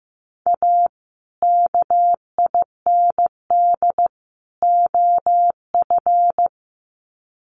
Morse code, heard as a keyed tone, about 15 wpm, 700 Hz. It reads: A KIND OF
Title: Sets of 3 Words 15wpm